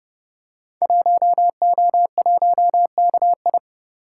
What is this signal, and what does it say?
Morse code, heard as a keyed tone, about 30 wpm, 700 Hz. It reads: 1O1KI
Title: Code Groups - 5 Letter:Numbers - 30wpm